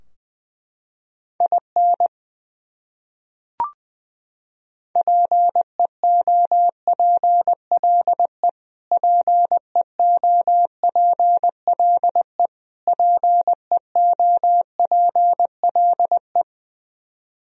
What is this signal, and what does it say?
Morse code, heard as a keyed tone, about 21 wpm, 700 Hz. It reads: IN PEOPLE PEOPLE PEOPLE
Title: Top 100 Words Repeated 20wpm